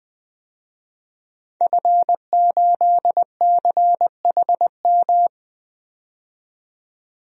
Morse code, heard as a keyed tone, about 20 wpm, 700 Hz. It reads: F8CHM
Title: DXCC-100 Call Signs - 5 Characters - 20wpm